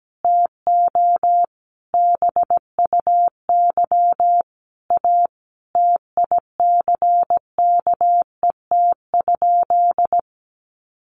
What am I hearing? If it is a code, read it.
Morse code, heard as a keyed tone, about 17 wpm, 700 Hz. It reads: TO BUY A TICKET?
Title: Sentences from Top 1400 Words 17wpm